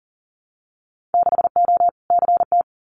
Morse code, heard as a keyed tone, about 40 wpm, 700 Hz. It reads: 6O CT
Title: Winter Field Day Exchanges Encore 40wpm